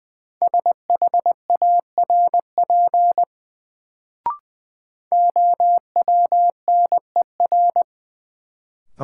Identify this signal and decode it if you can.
Morse code, heard as a keyed tone, about 20 wpm, 700 Hz. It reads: SHARP OWNER
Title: Sentences from Top 1600 Words - New Word Drill - 20wpm